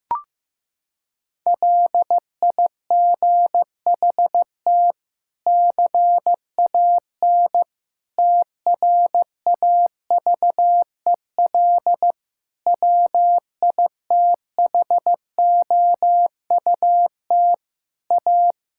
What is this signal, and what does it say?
Morse code, heard as a keyed tone, about 15 wpm, 700 Hz. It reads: LIGHT CAN TRAVEL WITHOUT A
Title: Sentences from Top 1000 Words - Review Part 6 - 15wpm